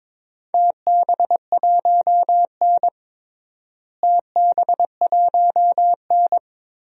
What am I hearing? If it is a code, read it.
Morse code, heard as a keyed tone, about 22 wpm, 700 Hz. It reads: TB1N TB1N
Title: DXCC-100 Call Signs - 4 Characters - Repeated 3x - 22wpm